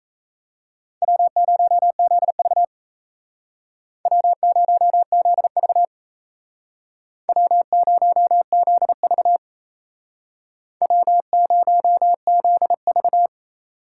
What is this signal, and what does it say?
Morse code, heard as a keyed tone, about 37 wpm, 700 Hz. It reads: W0ZV W0ZV W0ZV W0ZV